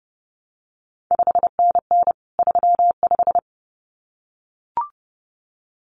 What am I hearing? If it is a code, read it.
Morse code, heard as a keyed tone, about 32 wpm, 700 Hz. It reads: 5NN 35
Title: CQ WW DX Contest Exchange Encore - 30wpm